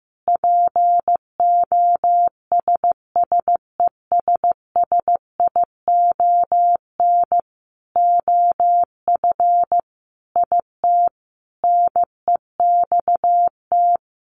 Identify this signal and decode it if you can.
Morse code, heard as a keyed tone, about 15 wpm, 700 Hz. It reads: POSSESSION OF IT NEXT
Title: Sentences from Top 1400 Words 15wpm